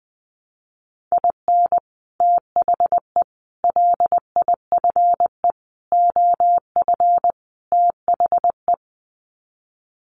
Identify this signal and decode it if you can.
Morse code, heard as a keyed tone, about 20 wpm, 700 Hz. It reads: IN THE LIFE OF THE
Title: Sets of 5 Words Encore2 20wpm